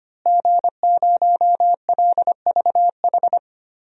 Morse code, heard as a keyed tone, about 25 wpm, 700 Hz. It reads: G0LVH
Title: DXCC-100 Call Signs - 5 Characters - Repeated 2x - 25wpm